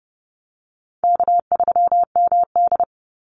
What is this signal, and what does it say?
Morse code, heard as a keyed tone, about 29 wpm, 700 Hz. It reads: K3MD
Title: Call Signs - 30wpm